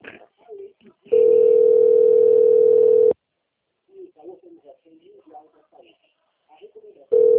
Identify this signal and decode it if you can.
Ringback tone (North America), 440 + 480 Hz, with speech.
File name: out- - - - - .27707